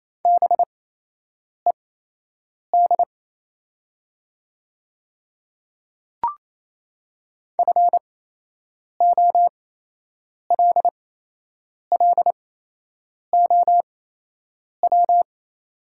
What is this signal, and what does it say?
Morse code, heard as a keyed tone, about 28 wpm, spaced as for 8 wpm, 700 Hz. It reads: BED FOLLOW
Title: 8x Character Spacing - Top 500 Words - 28wpm